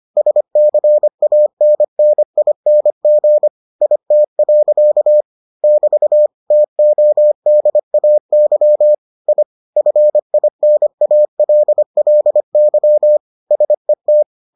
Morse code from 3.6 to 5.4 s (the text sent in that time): IT.